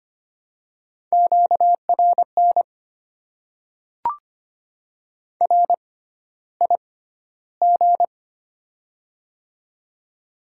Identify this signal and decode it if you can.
Morse code, heard as a keyed tone, about 25 wpm, spaced as for 13 wpm, 700 Hz. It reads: QRN RIG
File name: Top 90 QSO Elements - 6x Character Spacing - 25wpm